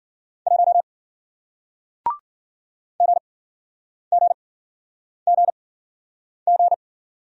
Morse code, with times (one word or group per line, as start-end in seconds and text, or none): 0.5-0.8 1
3.0-6.7 GGGG